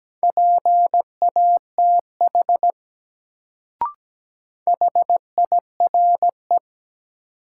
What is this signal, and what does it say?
Morse code, heard as a keyed tone, about 17 wpm, 700 Hz. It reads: PATH HIRE
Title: Sentences from Top 1400 Words - New Word Drill Alt - 17wpm